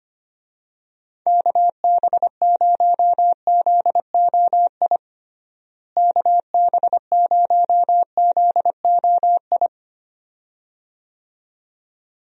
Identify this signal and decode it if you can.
Morse code, heard as a keyed tone, about 25 wpm, 700 Hz. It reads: KB0ZOI KB0ZOI
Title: USA Call Signs - 6 Characters - Repeated 2x - 25wpm